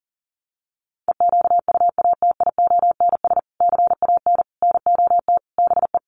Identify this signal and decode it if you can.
Morse code, heard as a keyed tone, about 40 wpm, 700 Hz. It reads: EQUATIONS CAN NOT BE
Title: Sets of 5 Words 40wpm